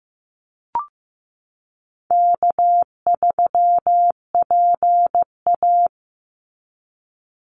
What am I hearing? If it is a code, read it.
Morse code, heard as a keyed tone, about 15 wpm, 700 Hz. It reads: K3PA
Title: Easy Call Signs Repeated 2x - 15wpm